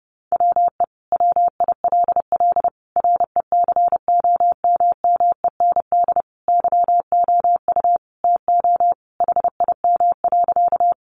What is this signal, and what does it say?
Morse code, heard as a keyed tone, about 30 wpm, 700 Hz. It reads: WE WILL RECOMMEND YOU TO HIM.
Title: Sentences from Top 1200 Words 30wpm